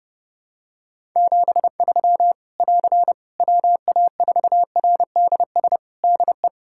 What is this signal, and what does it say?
Morse code, heard as a keyed tone, about 30 wpm, 700 Hz. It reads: 73 <AR> WA4RDS DE
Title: Practice QSO - Exchange 4 - Ending - 30wpm